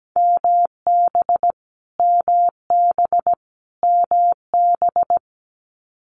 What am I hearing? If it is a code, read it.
Morse code, heard as a keyed tone, about 17 wpm, 700 Hz. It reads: MB MB MB
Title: Canadian Provinces and Territories - Repeated 3x - 17wpm